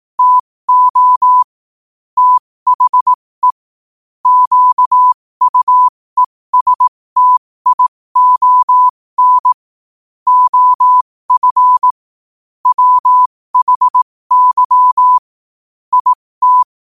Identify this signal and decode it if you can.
Morse code, heard as a keyed tone, about 18 wpm, spaced as for 15 wpm, 1000 Hz. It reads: TO THE QUESTION OF WHY IT
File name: qotdfiles / 15wpm